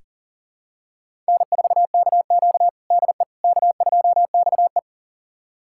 Morse code, heard as a keyed tone, about 40 wpm, 700 Hz. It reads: N4KQ DE K2XE
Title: USA Callsign DE Callsign - 4 Characters - 40wpm